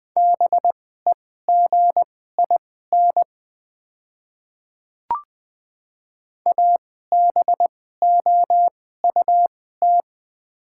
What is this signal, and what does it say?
Morse code, heard as a keyed tone, about 20 wpm, spaced as for 14 wpm, 690 Hz. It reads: BEGIN ABOUT
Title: 2x Character Spacing - Top 300 Words - 20wpm